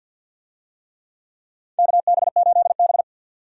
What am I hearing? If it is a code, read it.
Morse code, heard as a keyed tone, about 52 wpm, 700 Hz. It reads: KB8B
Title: USA Call Signs - 4 Characters - 50wpm